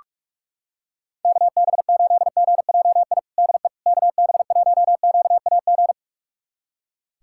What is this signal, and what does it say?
Morse code, heard as a keyed tone, about 45 wpm, 700 Hz. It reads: KB8GJI DE KB1QAG
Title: USA Callsign DE Callsign 45wpm